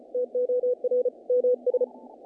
Morse code, heard as a keyed tone, about 35 wpm, 500 Hz. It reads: TOR MS
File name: contests / naqp-cw-k5tr-op-k5pi- -1800